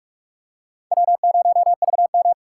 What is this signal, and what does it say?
Morse code, heard as a keyed tone, about 40 wpm, 700 Hz. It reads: W0VM